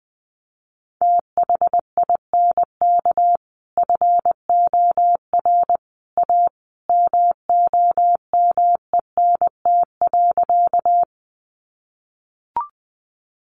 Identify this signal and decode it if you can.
Morse code, heard as a keyed tone, about 20 wpm, 700 Hz. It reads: THINK FOR A MOMENT.
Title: Sentences from Top 1000 Words 20wpm